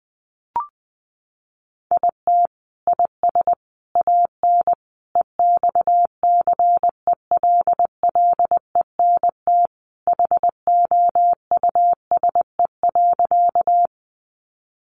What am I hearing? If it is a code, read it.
Morse code, heard as a keyed tone, about 20 wpm, 700 Hz. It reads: IT IS AN EXCELLENT HOUSE.
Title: Sentences from Top 1000 Words 20wpm